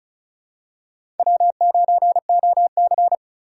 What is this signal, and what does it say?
Morse code, heard as a keyed tone, about 36 wpm, 700 Hz. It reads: W9OC
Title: USA Call Signs - 4 Characters - 3x Character Spacing - 35wpm